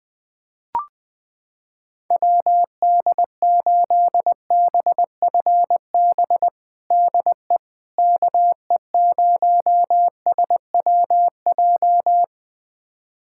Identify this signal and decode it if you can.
Morse code, heard as a keyed tone, about 20 wpm, 700 Hz. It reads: WD8BFB DE KE0SWJ
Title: USA Callsign DE Callsign - 6 Characters - 20wpm